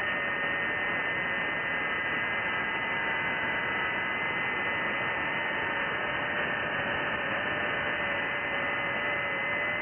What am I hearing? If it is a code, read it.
FT8.
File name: IC-7100